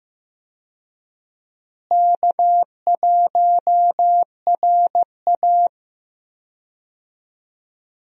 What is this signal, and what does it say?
Morse code, heard as a keyed tone, about 15 wpm, 700 Hz. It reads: K1RA
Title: USA Call Signs - 4 Characters - Repeated 2x - 15wpm